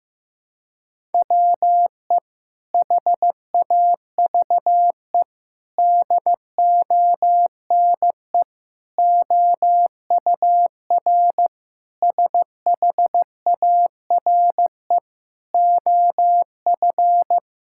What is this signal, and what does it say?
Morse code, heard as a keyed tone, about 15 wpm, 700 Hz. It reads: WE HAVE DONE OUR SHARE OF
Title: Sentences from Top 1000 Words - Review Part 6 - 15wpm